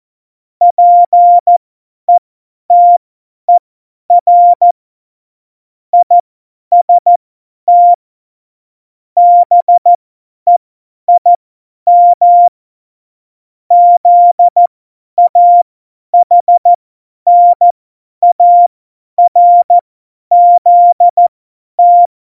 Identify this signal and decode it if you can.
Morse code, heard as a keyed tone, about 14 wpm, spaced as for 10 wpm, 700 Hz. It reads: PETER IST BEIM ZAHNARZT
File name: Morsekurs / Witze / Witz 02. Januar 2018, 50 F70